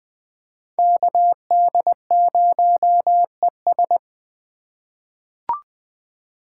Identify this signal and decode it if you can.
Morse code, heard as a keyed tone, about 20 wpm, 700 Hz. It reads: KD0ES
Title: USA Call Signs - 5 Characters - 6x Character Spacing - 20wpm